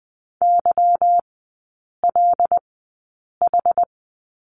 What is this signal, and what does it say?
Morse code, heard as a keyed tone, about 20 wpm, spaced as for 8 wpm, 700 Hz. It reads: YLH
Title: 178 Lesson 29e - Slash - Three Characters 20wpm